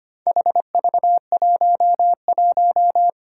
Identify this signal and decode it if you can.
Morse code, heard as a keyed tone, about 25 wpm, 700 Hz. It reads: HV11
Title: Four Challenging Letter-Number Combinations 25wpm